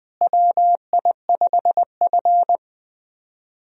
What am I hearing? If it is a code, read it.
Morse code, heard as a keyed tone, about 20 wpm, 700 Hz. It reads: WI5F
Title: USA Call Signs - 4 Characters - Repeated 3x - 20wpm